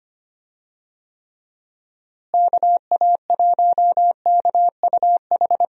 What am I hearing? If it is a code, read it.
Morse code, heard as a keyed tone, about 25 wpm, 700 Hz. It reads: KA1KUH
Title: USA Call Signs - 6 Characters - Repeated 3x - 25wpm